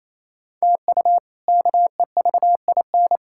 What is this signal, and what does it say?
Morse code, heard as a keyed tone, about 28 wpm, 700 Hz. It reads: TU KEVIN
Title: K1USN SST QSO Practice - 28wpm